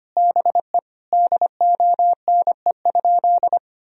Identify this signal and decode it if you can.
Morse code, heard as a keyed tone, about 25 wpm, 700 Hz. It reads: BE DONE?
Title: Sentences from Top 1000 Words - Review Part 4 - 25wpm